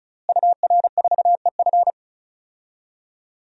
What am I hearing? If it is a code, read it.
Morse code, heard as a keyed tone, about 36 wpm, 700 Hz. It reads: UR4EF
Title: DXCC-100 Call Signs - 5 Characters - Repeated 2x - 35wpm